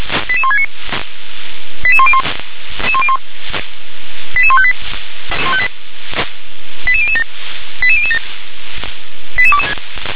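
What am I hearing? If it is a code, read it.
Five-tone selective call (ZVEI), heard as quick runs of five tones, tones of about 70 ms. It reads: 88169 88101 88158 88007 88007
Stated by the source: ZVEI five-tone call received by SDR.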